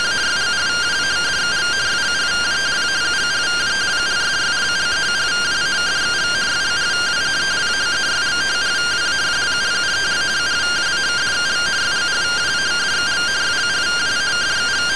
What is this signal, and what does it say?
Two-tone teleprinter signal, about 170 Hz shift.